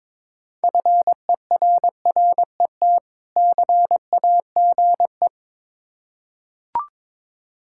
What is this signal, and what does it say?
Morse code, heard as a keyed tone, about 22 wpm, 690 Hz. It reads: FERRET CAGE
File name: Difficult Words with Hints - 22wpm